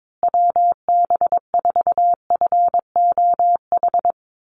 Morse code, heard as a keyed tone, about 22 wpm, 700 Hz. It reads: WB4FOH
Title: USA Call Signs Repeated 3x 22wpm